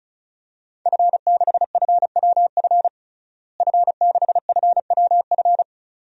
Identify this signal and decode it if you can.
Morse code, heard as a keyed tone, about 35 wpm, 700 Hz. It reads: F6FWF F6FWF
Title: DXCC-100 Call Signs - 5 Characters - Repeated 2x - 35wpm